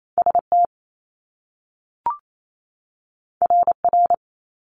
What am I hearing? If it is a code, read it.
Morse code, heard as a keyed tone, about 28 wpm, 700 Hz. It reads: ST RR
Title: Top Two Letter Combinations in Words 28wpm